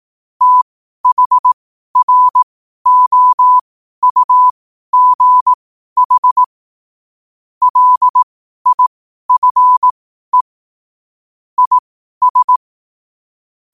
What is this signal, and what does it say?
Morse code, heard as a keyed tone, about 18 wpm, spaced as for 12 wpm, 1000 Hz. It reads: THROUGH LIFE IS
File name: qotdfiles / 12wpm